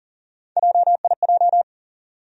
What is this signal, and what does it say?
Morse code, heard as a keyed tone, about 40 wpm, 700 Hz. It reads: JIJ
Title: USA Call Sign Postfixes - 3 Letters 40wpm